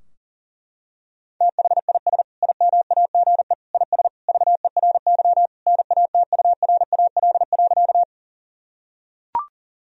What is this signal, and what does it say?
Morse code, heard as a keyed tone, about 40 wpm, 700 Hz. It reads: THIS IMAGE IS VERY NATURAL.
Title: Sentences from Top 1000 Words - Review Part 4 - 40wpm